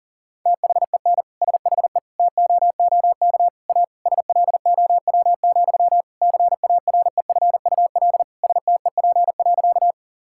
Morse code, heard as a keyed tone, about 37 wpm, 700 Hz. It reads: THEN SHE TOOK A SLOW, CAREFUL STEP.
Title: Sentences from Top 1600 Words 40wpm